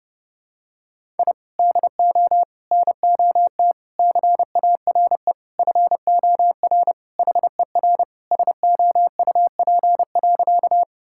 Morse code, heard as a keyed tone, about 30 wpm, 700 Hz. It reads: I DO NOT CARE FOR HER SOUP.
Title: Sentences from Top 1000 Words - Review Part 6 - 30wpm